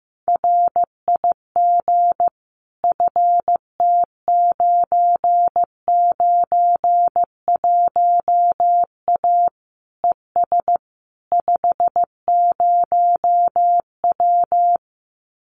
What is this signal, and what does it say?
Morse code, heard as a keyed tone, about 15 wpm, 700 Hz. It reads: RIG FT991A ES 50W
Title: Common QSO Phrases Encore 15wpm